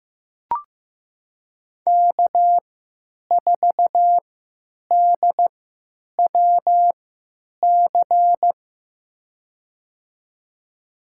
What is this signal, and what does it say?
Morse code, heard as a keyed tone, about 15 wpm, spaced as for 9 wpm, 700 Hz. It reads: K4DWC
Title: USA Call Signs - 5 Characters - 3x Character Spacing - 15wpm